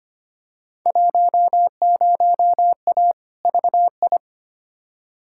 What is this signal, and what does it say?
Morse code, heard as a keyed tone, about 25 wpm, 700 Hz. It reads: 10A VI
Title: Field Day Exchanges Encore - 25wpm